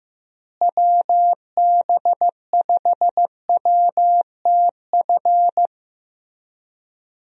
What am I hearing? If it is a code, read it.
Morse code, heard as a keyed tone, about 15 wpm, 700 Hz. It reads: WB5WTF
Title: USA Call Signs - 6 Characters - Repeated 2x - 15wpm